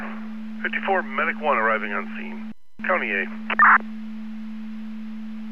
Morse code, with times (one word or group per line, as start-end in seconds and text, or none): none